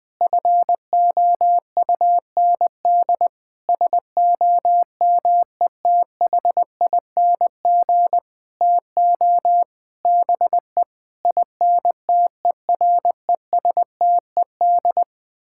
Morse code, heard as a keyed tone, about 21 wpm, 700 Hz. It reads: FOUND SOMETHING TO BE INTERESTED
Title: Sentences from Top 1000 Words - Review Part 6 - 20wpm